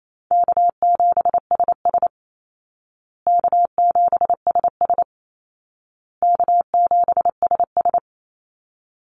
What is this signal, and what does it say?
Morse code, heard as a keyed tone, about 28 wpm, 700 Hz. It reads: K7SS K7SS K7SS
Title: Call Signs Repeated 2x - 28wpm